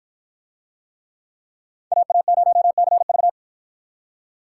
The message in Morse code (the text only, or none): AA0CV